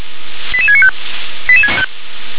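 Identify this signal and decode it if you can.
Five-tone selective call (ZVEI), heard as quick runs of five tones, tones of about 70 ms. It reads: 88695
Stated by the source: ZVEI five-tone selective call received by SDR.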